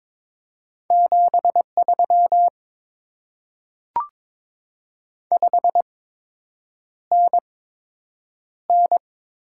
Morse code, heard as a keyed tone, about 23 wpm, spaced as for 8 wpm, 700 Hz. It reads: 73 5NN
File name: Top 20 QSO Elements - 8x Character Spacing - 22wpm